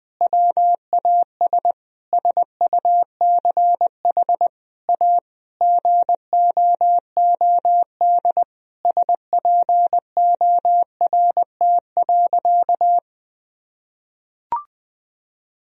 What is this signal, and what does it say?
Morse code, heard as a keyed tone, about 21 wpm, 700 Hz. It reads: WAS SUCH A GOOD SPORT.
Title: Sentences from Top 1000 Words - Review Part 4 - 20wpm